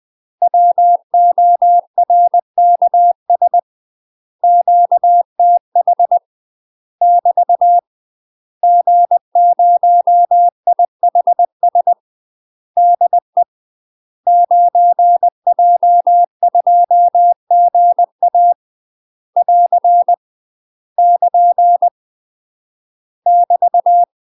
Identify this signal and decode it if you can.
Morse code, heard as a keyed tone, about 20 wpm, 700 Hz. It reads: WORKS QTH <BT> G0IHS DE 9J2GA <AR> <KN> <BT>
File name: UK MASTER QSOs TRAINING / 20WPM / UK MASTER QSOs 0035